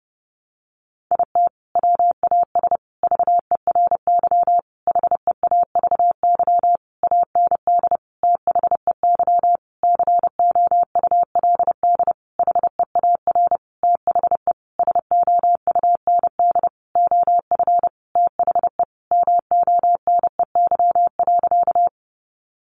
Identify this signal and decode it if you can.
Morse code, heard as a keyed tone, about 30 wpm, 700 Hz. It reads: IT WAS VERY HEAVY AND THEY COULD HEAR THE SOUND OF THE MONEY.
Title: Sentences from Top 1100 Words 30wpm